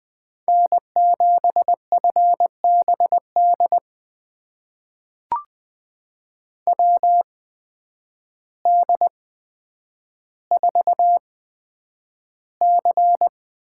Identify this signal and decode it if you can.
Morse code, heard as a keyed tone, about 20 wpm, spaced as for 9 wpm, 700 Hz. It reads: N7FBD WD4C
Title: USA Call Signs - 8x Character Spacing - 20wpm